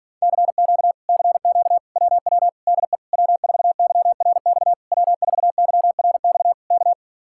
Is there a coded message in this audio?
Yes: Morse code from 0.2 to 6.9 s, heard as a keyed tone, about 48 wpm, 700 Hz. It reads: CQ CQ WW DE W4YRX W4YRX K